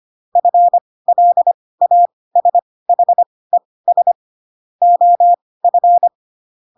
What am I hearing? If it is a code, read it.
Morse code, heard as a keyed tone, about 25 wpm, spaced as for 18 wpm, 700 Hz. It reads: FLASHES OF
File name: Pinocchio / Pinocchio ch6 12 25 ik1hge